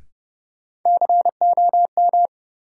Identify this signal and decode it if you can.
Morse code, heard as a keyed tone, about 30 wpm, 700 Hz. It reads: COM
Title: Sentences from Top 1300 Words - New Word Drill Alt - 30wpm